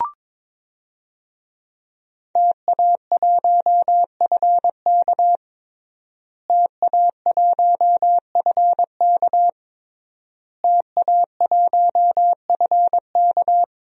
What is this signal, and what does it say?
Morse code, heard as a keyed tone, about 21 wpm, 700 Hz. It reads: TA1FK TA1FK TA1FK
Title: DXCC-100 Call Signs - 5 Characters - Repeated 3x - 22wpm